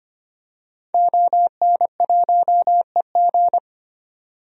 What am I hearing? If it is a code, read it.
Morse code, heard as a keyed tone, about 25 wpm, 700 Hz. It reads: ON1EG
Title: DXCC-100 Call Signs - 5 Characters - 25wpm